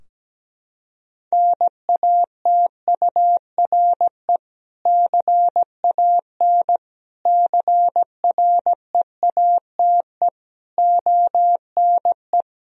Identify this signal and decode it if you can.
Morse code, heard as a keyed tone, about 17 wpm, 700 Hz. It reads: NATURE CAN CREATE ONE
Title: Sentences from Top 1000 Words - Review Part 3 - 17wpm